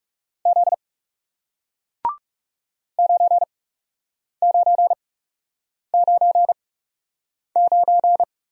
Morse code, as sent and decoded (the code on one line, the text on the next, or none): --.. / ----. ----. ----. ----.
Z 9999